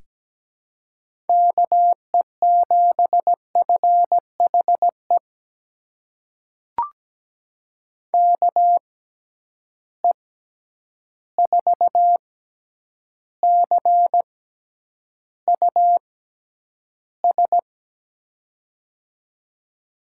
Morse code, heard as a keyed tone, about 17 wpm, spaced as for 9 wpm, 700 Hz. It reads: KE7FHE KE4CUS
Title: USA Call Signs - 6 Characters - 6x Character Spacing - 17wpm